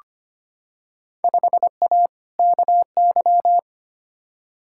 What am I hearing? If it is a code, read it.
Morse code, heard as a keyed tone, about 25 wpm, 700 Hz. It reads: 5A KY
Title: Field Day Exchanges - 25wpm